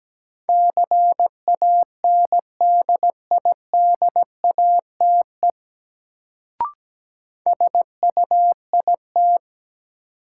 Morse code, heard as a keyed tone, about 17 wpm, 700 Hz. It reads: CANDIDATE SUIT
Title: Sentences from Top 1600 Words - New Word Drill Alt - 17wpm